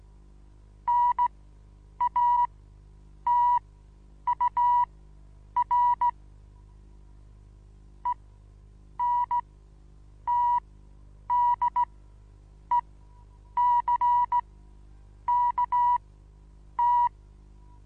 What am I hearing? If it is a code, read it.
Morse code, heard as a keyed tone, about 16 wpm, spaced as for 8 wpm, 970 Hz. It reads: NATUR ENTDECKT
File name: Onlinelehrgang / morsen / DJ4UF Morsen25